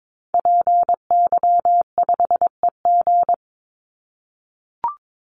Morse code, heard as a keyed tone, about 22 wpm, 700 Hz. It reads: PY5EG
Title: Call Signs - 22wpm